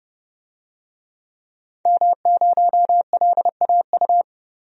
Morse code, heard as a keyed tone, about 30 wpm, 700 Hz. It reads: M0LAU
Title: DXCC-100 Call Signs - 5 Characters - 30wpm